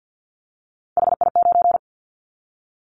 Morse code, heard as a keyed tone, about 50 wpm, 700 Hz. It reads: HI9
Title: Callsign Prefixes for Top 100 DXCC - 50wpm